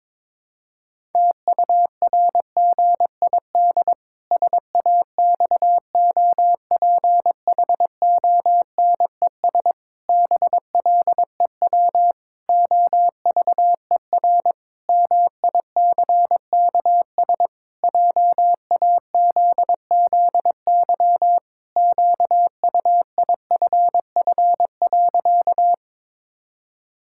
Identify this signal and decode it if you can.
Morse code, heard as a keyed tone, about 21 wpm, 700 Hz. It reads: TURGID SAXOPHONES BLEW OVER MICKS JAZZY QUIFF.
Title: Pangram Challenge v2 - 22wpm